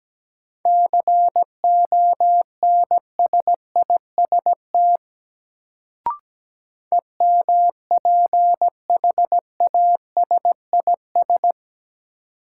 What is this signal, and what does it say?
Morse code, heard as a keyed tone, about 17 wpm, 700 Hz. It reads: CONSIST EMPHASIS
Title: Sentences from Top 1400 Words - New Word Drill - 17wpm